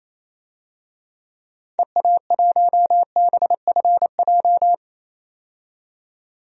Morse code, heard as a keyed tone, about 28 wpm, 700 Hz. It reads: EA1BFJ
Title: DXCC-100 Call Signs - 6 Characters - 28wpm